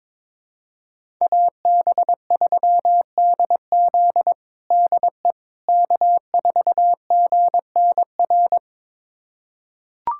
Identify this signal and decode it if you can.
Morse code, heard as a keyed tone, about 22 wpm, 700 Hz. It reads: AB3DZ DE K4GNR
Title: USA Callsign DE Callsign - 5 Characters - 22wpm